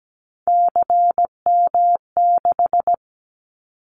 Morse code, heard as a keyed tone, about 17 wpm, 700 Hz. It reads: CM6
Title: Callsign Prefixes for Top 100 DXCC - 17wpm